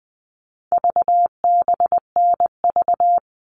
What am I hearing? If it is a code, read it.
Morse code, heard as a keyed tone, about 20 wpm, 700 Hz. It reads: VBNV
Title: All Cut Numbers - 4-Digit 20wpm